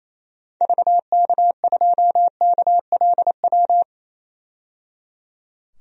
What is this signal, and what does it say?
Morse code, heard as a keyed tone, about 28 wpm, 700 Hz. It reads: VK2KLW
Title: DXCC-100 Call Signs - 6 Characters - 28wpm